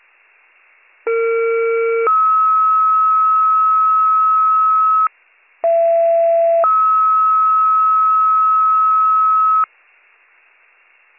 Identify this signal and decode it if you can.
Two-tone page, heard as a one-second tone, then a three-second tone, about 460 then 1290 Hz, about 670 then 1290 Hz. It is heard from 1.1 to 9.6 s.